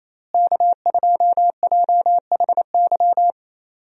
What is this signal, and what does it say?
Morse code, heard as a keyed tone, about 28 wpm, 700 Hz. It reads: K2JHY
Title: USA Call Signs - 5 Characters - Repeated 2x - 28wpm